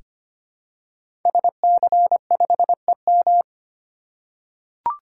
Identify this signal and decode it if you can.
Morse code, heard as a keyed tone, about 25 wpm, 700 Hz. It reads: SC5EM
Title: Code Groups - 5 Letter:Numbers - 25wpm